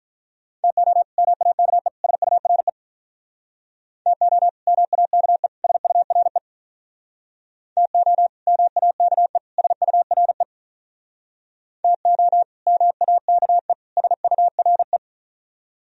Morse code, heard as a keyed tone, about 45 wpm, 700 Hz. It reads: TO MAKE SURE TO MAKE SURE TO MAKE SURE TO MAKE SURE